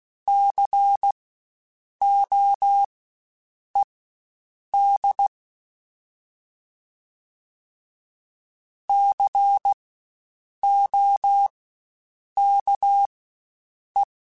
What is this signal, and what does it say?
Morse code, heard as a keyed tone, about 16 wpm, spaced as for 7 wpm, 800 Hz. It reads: COED COKE
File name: Spaced Alphas / Spaced Alphas mp3s / Cs sp @15